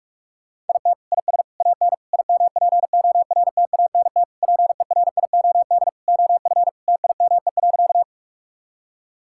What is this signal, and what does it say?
Morse code, heard as a keyed tone, about 46 wpm, 700 Hz. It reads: IT IS AN IMPORTANT PERIOD OF TIME.
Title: Sentences from Top 1000 Words - Review Part 5 - 45wpm